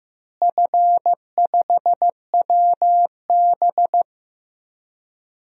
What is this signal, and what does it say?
Morse code, heard as a keyed tone, about 15 wpm, 700 Hz. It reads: F5WB
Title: DXCC-100 Call Signs - 4 Characters - Repeated 2x - 15wpm